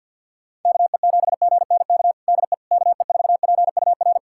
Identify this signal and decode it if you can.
Morse code, heard as a keyed tone, about 49 wpm, 700 Hz. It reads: KE7GNK DE KE4PUR
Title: USA Callsign DE Callsign - 6 Characters - 50wpm